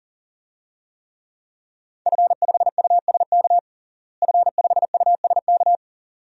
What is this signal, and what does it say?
Morse code, heard as a keyed tone, about 37 wpm, 700 Hz. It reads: F5USK F5USK
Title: DXCC-100 Call Signs - 5 Characters - Repeated 2x - 40wpm